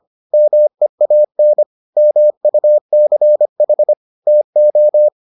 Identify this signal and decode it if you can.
Morse code, heard as a keyed tone, about 25 wpm, 600 Hz. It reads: MEAN MUCH TO